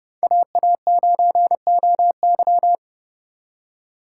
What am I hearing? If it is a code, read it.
Morse code, heard as a keyed tone, about 36 wpm, 700 Hz. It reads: AA9OY
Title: Speed Racing - USA 2x2 Call Signs 30wpm